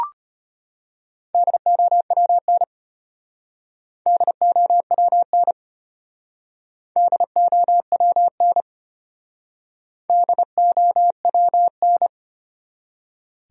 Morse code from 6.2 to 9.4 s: DOWN

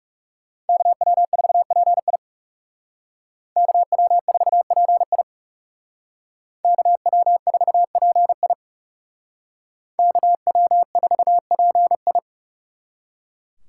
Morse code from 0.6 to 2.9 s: KW4PI

